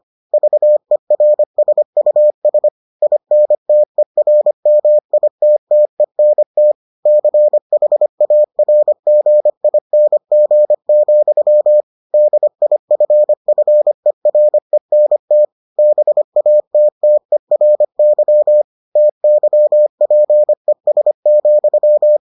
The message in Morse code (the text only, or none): VERSUS INTERMITTENT CHARGING, DIFFERENT BATTERY TYPES,